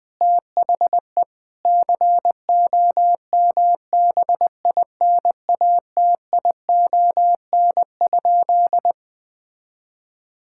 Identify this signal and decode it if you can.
Morse code, heard as a keyed tone, about 21 wpm, 700 Hz. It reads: THE COMBINATION?
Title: Sentences from Top 1000 Words - Review Part 6 - 20wpm